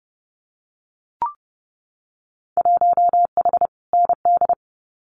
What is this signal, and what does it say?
Morse code, heard as a keyed tone, about 30 wpm, 700 Hz. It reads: 1H ND
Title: Winter Field Day Exchanges 30wpm